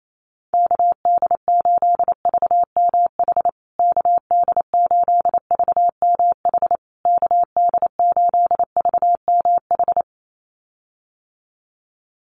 Morse code, heard as a keyed tone, about 28 wpm, 700 Hz. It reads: KD8VMH KD8VMH KD8VMH
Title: USA Call Signs Repeated 3x 28wpm